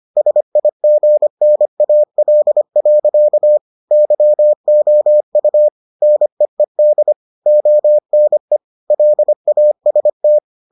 Morse code, heard as a keyed tone, about 25 wpm, 600 Hz. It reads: SIGNAL. YOU NEED ONE LAST